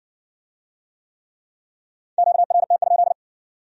Morse code, heard as a keyed tone, about 53 wpm, 700 Hz. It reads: QRT?